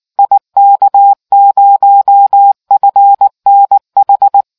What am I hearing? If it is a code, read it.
Morse code, heard as a keyed tone, about 19 wpm, 800 Hz. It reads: IK0FNH